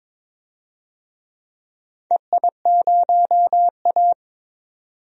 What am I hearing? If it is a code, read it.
Morse code, heard as a keyed tone, about 22 wpm, 700 Hz. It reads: EI0A
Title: DXCC-100 Call Signs - 4 Characters - 22wpm